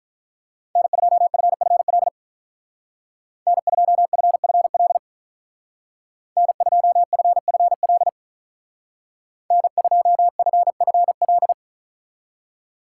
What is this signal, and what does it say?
Morse code, heard as a keyed tone, about 43 wpm, 700 Hz. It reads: N2FFL N2FFL N2FFL N2FFL
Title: Speed Racing - USA 1x3 Call Signs 35wpm